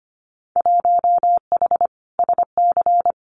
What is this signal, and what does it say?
Morse code, heard as a keyed tone, about 25 wpm, 690 Hz. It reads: 1H SC
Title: Winter Field Day Exchanges 25wpm